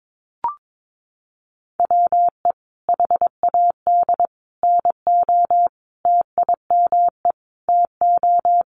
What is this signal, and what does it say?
Morse code, heard as a keyed tone, about 22 wpm, 700 Hz. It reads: WE HAD NO TIME TO
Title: Sentences from Top 1100 Words 22wpm